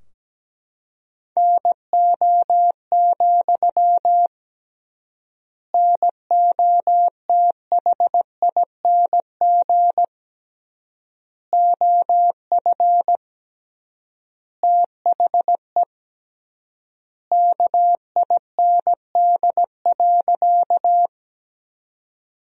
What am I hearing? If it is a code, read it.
Morse code, heard as a keyed tone, about 17 wpm, 700 Hz. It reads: NO, NOTHING OF THE KIND.
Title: 3x Word-Spacing - Sentences from Top 500 Words - 17wpm